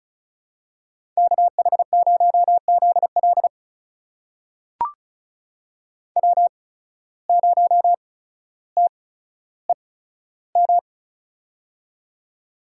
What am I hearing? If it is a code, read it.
Morse code, heard as a keyed tone, about 36 wpm, spaced as for 15 wpm, 700 Hz. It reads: KH0ZL W0TEM
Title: USA Call Signs - 5 Characters - 8x Character Spacing - 35wpm